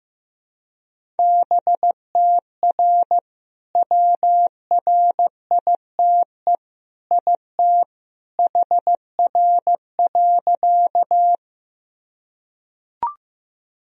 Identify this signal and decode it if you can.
Morse code, heard as a keyed tone, about 16 wpm, 700 Hz. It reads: BTR WRITE IT HR.
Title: Sentences from Top 400 Words and Abbreviations 15wpm